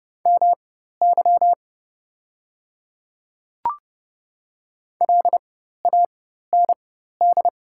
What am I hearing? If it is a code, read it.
Morse code, heard as a keyed tone, about 30 wpm, spaced as for 14 wpm, 700 Hz. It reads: MY LAND
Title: 4x Character Spacing - Top 200 Words - 30wpm